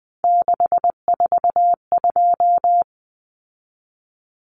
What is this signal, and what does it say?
Morse code, heard as a keyed tone, about 20 wpm, 700 Hz. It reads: 642
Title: Numbers - 3-Digit 20wpm